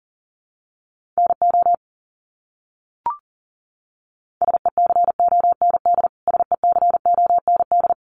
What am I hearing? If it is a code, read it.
Morse code, heard as a keyed tone, about 40 wpm, 700 Hz. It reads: NO SECOND SECOND
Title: Top 500 Words Repeated 40wpm